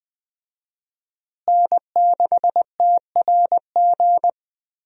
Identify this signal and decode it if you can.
Morse code, heard as a keyed tone, about 21 wpm, 700 Hz. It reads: N6TRG
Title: USA Call Signs - 5 Characters - Repeated 3x - 20wpm